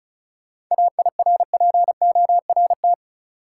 Morse code, heard as a keyed tone, about 36 wpm, 700 Hz. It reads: AIRPORT
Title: Seven Letter Words - 35wpm